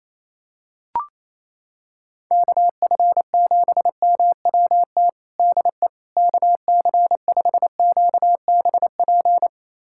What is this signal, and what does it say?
Morse code, heard as a keyed tone, about 28 wpm, 700 Hz. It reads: KF7MWT DE KC5QBP
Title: USA Callsign DE Callsign - 6 Characters - 28wpm